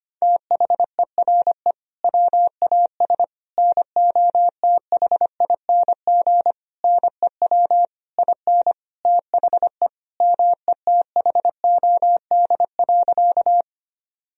Morse code, heard as a keyed tone, about 25 wpm, 700 Hz. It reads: THERE WAS NOTHING NEW IN THE METHOD.
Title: Sentences from Top 1000 Words - Review Part 3 - 25wpm